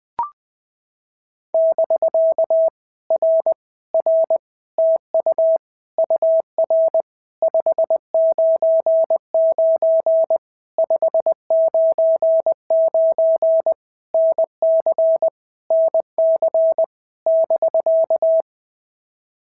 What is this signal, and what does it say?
Morse code, heard as a keyed tone, about 20 wpm, 650 Hz. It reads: <BK> R R TU UR 599 599 NC NC <BK>
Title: POTA QSO Practice - 20wpm